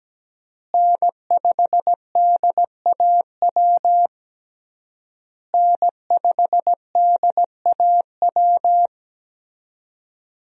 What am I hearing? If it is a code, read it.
Morse code, heard as a keyed tone, about 17 wpm, 700 Hz. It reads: N5DAW N5DAW
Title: USA Call Signs - 5 Characters - Repeated 3x - 17wpm